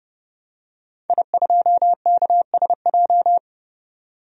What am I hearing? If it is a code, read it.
Morse code, heard as a keyed tone, about 30 wpm, 700 Hz. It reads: I2KSJ
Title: DXCC-100 Call Signs - 5 Characters - 30wpm